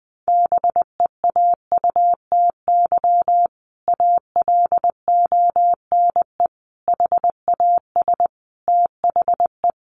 Morse code, heard as a keyed tone, about 20 wpm, 700 Hz. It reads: BEAUTY ALONE HAS THE
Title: Sentences from Top 1200 Words 20wpm